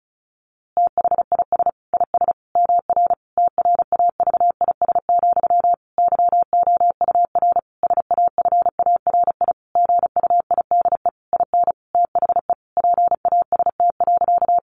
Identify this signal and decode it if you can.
Morse code, heard as a keyed tone, about 34 wpm, 700 Hz. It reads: THIS IS MR TRAVIS, YOUR SAFARI GUIDE IN THE PAST.
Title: A Sound Of Thunder 35wpm